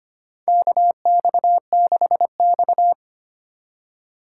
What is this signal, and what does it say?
Morse code, heard as a keyed tone, about 25 wpm, 700 Hz. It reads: KX6X
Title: K1USN SST QSO Practice - 25wpm